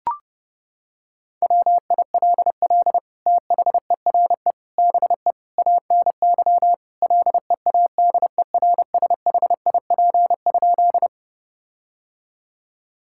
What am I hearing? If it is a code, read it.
Morse code, heard as a keyed tone, about 32 wpm, 700 Hz. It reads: WILL THERE BE ANY LEADERSHIP?
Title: Sentences from Top 1600 Words 30wpm